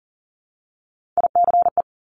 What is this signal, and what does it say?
Morse code, heard as a keyed tone, about 40 wpm, 700 Hz. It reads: ICE
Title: Multi-letter Phonemes Repeated 3x 40wpm